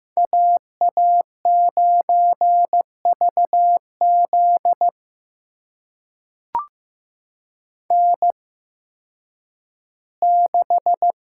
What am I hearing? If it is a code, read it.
Morse code, heard as a keyed tone, about 15 wpm, spaced as for 10 wpm, 700 Hz. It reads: AA9VZ N6
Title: USA Call Signs - 5 Characters - 8x Character Spacing - 15wpm